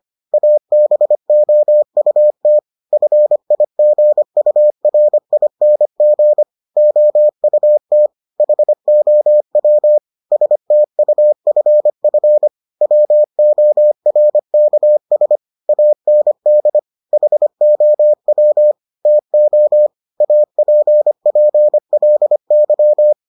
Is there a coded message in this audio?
Yes: Morse code from 0.3 to 23.1 s, heard as a keyed tone, about 25 wpm, 600 Hz. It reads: ABOUT FIGURING OUT HOW STUFF WORKS AND HOW TO APPLY